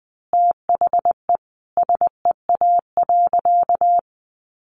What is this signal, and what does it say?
Morse code, heard as a keyed tone, about 20 wpm, 700 Hz. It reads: THE SEA.
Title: Sentences from Top 1500 Words 20wpm